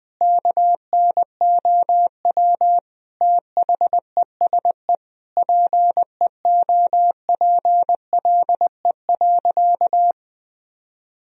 Morse code, heard as a keyed tone, about 20 wpm, 700 Hz. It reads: KNOW THESE PEOPLE.
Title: Sentences from Top 1000 Words - Review Part 1 - 20wpm